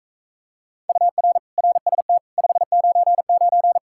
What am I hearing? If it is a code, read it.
Morse code, heard as a keyed tone, about 44 wpm, 700 Hz. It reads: UR RST 599